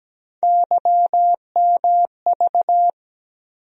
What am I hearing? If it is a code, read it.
Morse code, heard as a keyed tone, about 17 wpm, 700 Hz. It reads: YMV
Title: USA Call Sign Postfixes - 3 Letters 17wpm